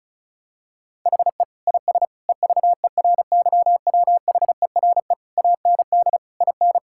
Morse code, heard as a keyed tone, about 36 wpm, 700 Hz. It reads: HE IS EVERYWHERE AND IN
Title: Sentences from Top 1600 Words 35wpm